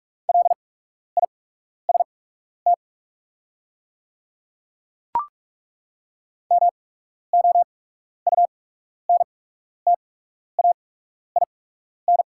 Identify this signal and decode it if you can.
Morse code, heard as a keyed tone, about 44 wpm, spaced as for 12 wpm, 690 Hz. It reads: LIST MOUNTAIN
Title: 8x Character Spacing - Top 500 Words - 45wpm